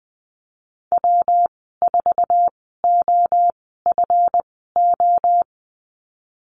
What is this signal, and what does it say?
Morse code, heard as a keyed tone, about 20 wpm, spaced as for 15 wpm, 700 Hz. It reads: W4OFO
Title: 140 Lesson 23g - Letter f - Call Signs 20wpm